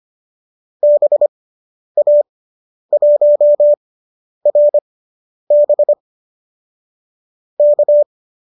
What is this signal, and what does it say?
Morse code, heard as a keyed tone, about 25 wpm, spaced as for 10 wpm, 570 Hz. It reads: BA1RB K